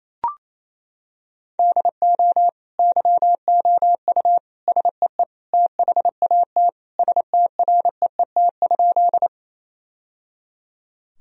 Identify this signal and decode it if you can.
Morse code, heard as a keyed tone, about 28 wpm, 700 Hz. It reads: DO YOU SEE THAT STREET?
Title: Sentences from Top 1000 Words - Review Part 3 - 28wpm